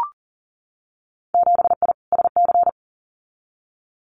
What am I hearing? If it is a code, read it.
Morse code, heard as a keyed tone, about 42 wpm, 700 Hz. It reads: 7I SC
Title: Winter Field Day Exchanges Encore 40wpm